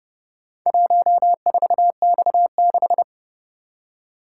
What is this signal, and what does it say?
Morse code, heard as a keyed tone, about 31 wpm, 700 Hz. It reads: 14X6